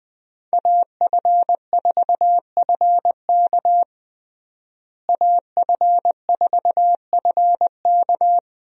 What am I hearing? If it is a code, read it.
Morse code, heard as a keyed tone, about 20 wpm, 700 Hz. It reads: AF4FK AF4FK
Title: USA Call Signs - 5 Characters - Repeated 3x - 20wpm